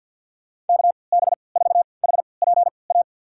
Morse code, heard as a keyed tone, about 49 wpm, spaced as for 28 wpm, 700 Hz. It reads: KB4HPA
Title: USA Call Signs - 6 Characters - 3x Character Spacing - 50wpm